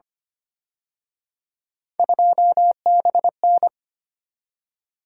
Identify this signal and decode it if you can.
Morse code, heard as a keyed tone, about 29 wpm, 700 Hz. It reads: 2BN
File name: Speed-Racing - Three Letter-Number Combinations - 25wpm